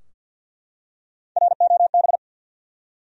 Morse code, heard as a keyed tone, about 53 wpm, 700 Hz. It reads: ROB
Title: USA Common Names - Max 4 Letters - 50wpm